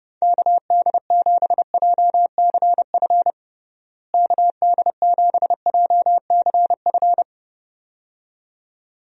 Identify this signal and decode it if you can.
Morse code, heard as a keyed tone, about 30 wpm, 700 Hz. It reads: KD7JCF KD7JCF
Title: USA Call Signs - 6 Characters - Repeated 2x - 30wpm